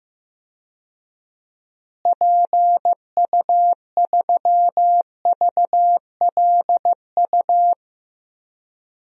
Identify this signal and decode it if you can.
Morse code, heard as a keyed tone, about 15 wpm, 700 Hz. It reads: PU3VLU
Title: DXCC-100 Call Signs - 6 Characters - Repeated 2x - 15wpm